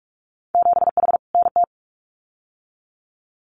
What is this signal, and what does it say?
Morse code, heard as a keyed tone, about 46 wpm, 700 Hz. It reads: 7H NT
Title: Winter Field Day Exchanges Encore 45wpm